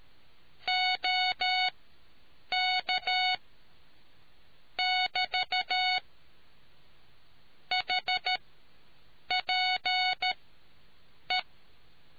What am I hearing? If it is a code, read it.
Morse code, heard as a keyed tone, about 13 wpm, spaced as for 7 wpm, 740 Hz. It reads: OK <BT> HPE